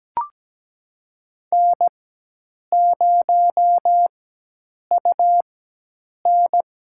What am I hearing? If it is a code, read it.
Morse code, heard as a keyed tone, about 17 wpm, spaced as for 8 wpm, 700 Hz. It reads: N0UN
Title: USA Call Signs - 4 Characters - 4x Character Spacing - 17wpm